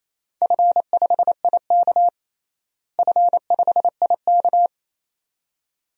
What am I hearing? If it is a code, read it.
Morse code, heard as a keyed tone, about 28 wpm, 700 Hz. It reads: F5IK F5IK
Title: DXCC-100 Call Signs - 4 Characters - Repeated 2x - 28wpm